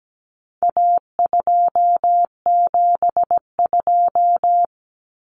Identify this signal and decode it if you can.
Morse code, heard as a keyed tone, about 17 wpm, 700 Hz. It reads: A272
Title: Common Cut Numbers - 4-Digit 17wpm